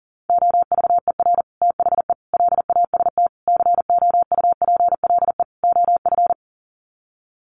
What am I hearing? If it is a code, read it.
Morse code, heard as a keyed tone, about 40 wpm, 700 Hz. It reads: OVER THE LAST COUPLE OF
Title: Sets of 5 Words Encore 40wpm